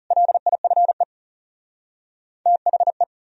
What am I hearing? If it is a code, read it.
Morse code, heard as a keyed tone, about 42 wpm, 700 Hz. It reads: LIFE THE